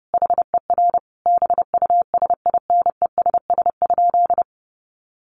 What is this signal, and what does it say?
Morse code, heard as a keyed tone, about 30 wpm, 700 Hz. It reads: HER BUSINESS?
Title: Sentences from Top 1100 Words 30wpm